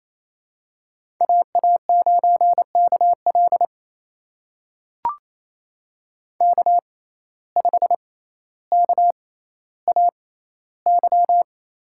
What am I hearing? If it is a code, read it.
Morse code, heard as a keyed tone, about 28 wpm, spaced as for 14 wpm, 700 Hz. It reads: AA9KL K5KAY
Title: USA Call Signs - 5 Characters - 6x Character Spacing - 28wpm